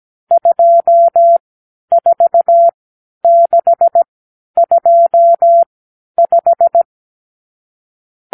Morse code, heard as a keyed tone, about 17 wpm, spaced as for 10 wpm, 680 Hz. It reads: 24625
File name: k7qo / Lesson 54 (9)